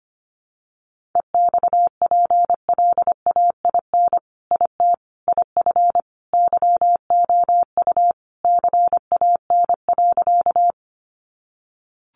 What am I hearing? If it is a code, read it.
Morse code, heard as a keyed tone, about 25 wpm, 700 Hz. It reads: EXPLAIN IT IF YOU CAN.
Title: Sentences from Top 1100 Words 25wpm